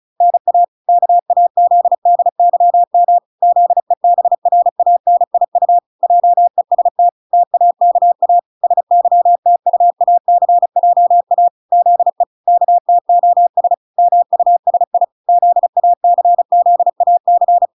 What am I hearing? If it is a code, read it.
Morse code, heard as a keyed tone, about 34 wpm, 700 Hz. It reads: NA KAZDYM ZEBRANIU JEST TAKA SYTUACJA ZE KTOS MUSI ZACZAC